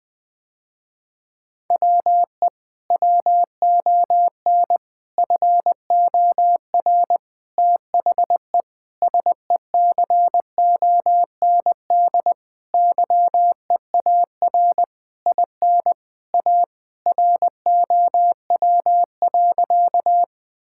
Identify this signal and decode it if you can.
Morse code, heard as a keyed tone, about 21 wpm, 700 Hz. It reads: WE WON FOR THE SECOND YEAR IN A ROW.
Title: Sentences with 6-Word Phrases - Spoken Beginning - 20wpm